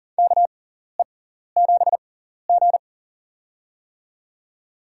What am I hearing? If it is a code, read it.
Morse code, heard as a keyed tone, about 40 wpm, spaced as for 14 wpm, 700 Hz. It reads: KE7G
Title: USA Call Signs - 6x Character Spacing - 40wpm